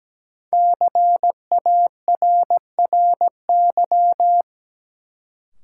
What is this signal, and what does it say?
Morse code, heard as a keyed tone, about 17 wpm, 700 Hz. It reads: CARRY
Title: Sentences from Top 400 Words - New Word Drill - 17wpm